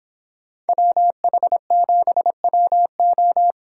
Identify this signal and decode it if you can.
Morse code, heard as a keyed tone, about 26 wpm, 700 Hz. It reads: WH7WO